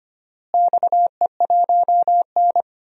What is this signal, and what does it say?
Morse code, heard as a keyed tone, about 25 wpm, 700 Hz. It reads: XE1N
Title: DXCC-100 Call Signs - 4 Characters - Repeated 2x - 25wpm